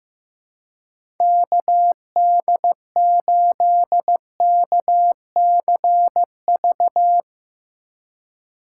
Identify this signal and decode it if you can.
Morse code, heard as a keyed tone, about 16 wpm, 700 Hz. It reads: KD8KCV
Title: USA Call Signs - 2x Character Spacing - 15wpm